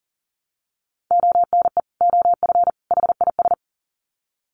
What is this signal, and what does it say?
Morse code, heard as a keyed tone, about 40 wpm, 700 Hz. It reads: ONE OF HIS
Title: Sets of 3 Words 40wpm